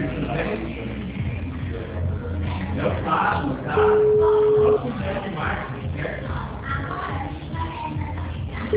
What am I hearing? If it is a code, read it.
Ringback tone (Europe, 425 Hz), with speech.
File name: out- -2008- - - .4592